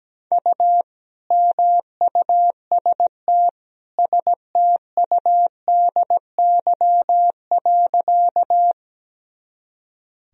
Morse code, heard as a keyed tone, about 17 wpm, 700 Hz. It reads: U MUST STUDY.
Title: Sentences from Top 300 Words and Abbreviations 17wpm